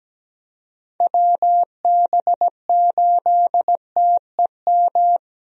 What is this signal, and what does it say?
Morse code, heard as a keyed tone, about 17 wpm, 700 Hz. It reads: WB8TEM
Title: USA Call Signs - 6 Characters - Repeated 3x - 17wpm